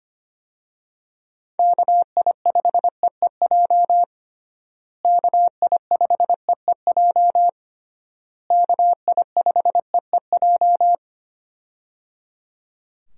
Morse code, heard as a keyed tone, about 25 wpm, 700 Hz. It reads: KI5EEJ KI5EEJ KI5EEJ
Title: USA Call Signs - 6 Characters - Repeated 3x - 25wpm